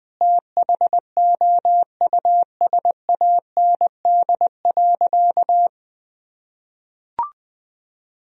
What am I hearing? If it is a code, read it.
Morse code, heard as a keyed tone, about 20 wpm, 700 Hz. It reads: THOUSAND.
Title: Sentences from Top 1000 Words - Review Part 3 - 20wpm